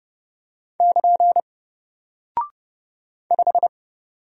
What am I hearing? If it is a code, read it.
Morse code, heard as a keyed tone, about 30 wpm, 700 Hz. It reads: <KN> 5
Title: Single Character (Letter:Number:Punctuation:Prosign) Alt 30wpm